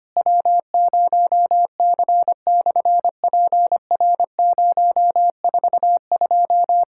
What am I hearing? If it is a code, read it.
Morse code, heard as a keyed tone, about 24 wpm, 700 Hz. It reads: W0C/PR042
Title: SOTA QSO Practice 25wpm